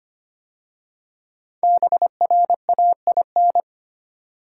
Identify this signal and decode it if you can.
Morse code, heard as a keyed tone, about 25 wpm, 700 Hz. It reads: BRAIN
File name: 5 Letter Words - First 3 Letters Given - 25wpm